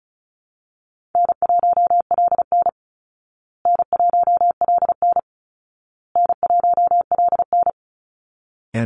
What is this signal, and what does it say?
Morse code, heard as a keyed tone, about 34 wpm, 700 Hz. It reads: N1LN N1LN N1LN
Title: Easy Call Signs Repeated 3x - 35wpm